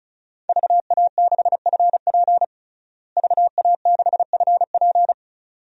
Morse code, heard as a keyed tone, about 36 wpm, 700 Hz. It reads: VA6FP VA6FP
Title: DXCC-100 Call Signs - 5 Characters - Repeated 2x - 35wpm